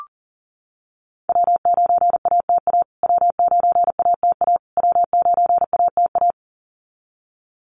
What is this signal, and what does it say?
Morse code, heard as a keyed tone, about 40 wpm, 700 Hz. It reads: W9ATA W9ATA W9ATA
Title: USA Call Signs Repeated 3x 40wpm